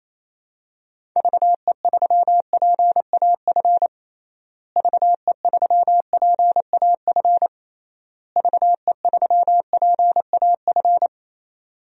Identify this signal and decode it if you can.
Morse code, heard as a keyed tone, about 28 wpm, 700 Hz. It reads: VE3PAF VE3PAF VE3PAF
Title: DXCC-100 Call Signs - 6 Characters - Repeated 3x - 28wpm